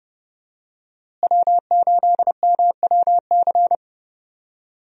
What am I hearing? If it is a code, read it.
Morse code, heard as a keyed tone, about 30 wpm, 700 Hz. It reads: W8MWC
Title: USA Call Signs - 5 Characters - Repeated 2x - 30wpm